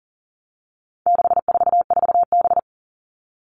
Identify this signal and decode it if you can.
Morse code, heard as a keyed tone, about 40 wpm, 700 Hz. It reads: 644B
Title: All Cut Numbers - 4-Digit 40wpm